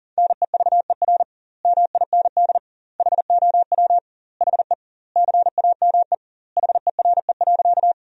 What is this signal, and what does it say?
Morse code, heard as a keyed tone, about 40 wpm, 700 Hz. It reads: NEVER MIND HOW HE CAME HERE.
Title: 2x Word-Spacing - Sentences from Top 500 Words - 40wpm